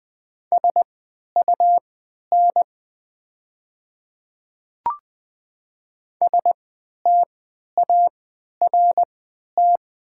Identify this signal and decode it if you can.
Morse code, heard as a keyed tone, about 20 wpm, spaced as for 11 wpm, 700 Hz. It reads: SUN START
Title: 3x Character Spacing - Top 500 Words - 20wpm